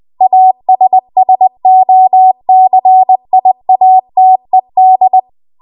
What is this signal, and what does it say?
Morse code, heard as a keyed tone, about 20 wpm, 750 Hz. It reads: ASSOCIATED
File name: Morse / 20WPM